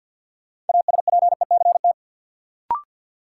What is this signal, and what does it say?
Morse code, heard as a keyed tone, about 51 wpm, 700 Hz. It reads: ASPECT
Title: Sentences from Top 1300 Words - New Word Drill Alt - 50wpm